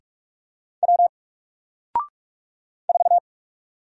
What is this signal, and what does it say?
Morse code, heard as a keyed tone, about 44 wpm, 700 Hz. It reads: W 4
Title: Single Character (Letter:Number:Punctuation:Prosign) 45wpm